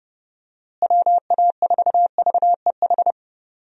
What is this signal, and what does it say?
Morse code, heard as a keyed tone, about 32 wpm, 700 Hz. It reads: WA4VEH
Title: USA Call Signs - 6 Characters - Repeated 2x - 30wpm